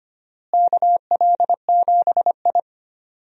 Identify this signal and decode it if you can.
Morse code, heard as a keyed tone, about 25 wpm, 700 Hz. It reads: KL7I
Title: DXCC-100 Call Signs - 4 Characters - Repeated 2x - 25wpm